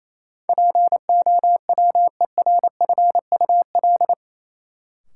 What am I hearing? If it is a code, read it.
Morse code, heard as a keyed tone, about 28 wpm, 700 Hz. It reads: POWERFUL
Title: Sentences from Top 1300 Words - New Word Drill Alt - 28wpm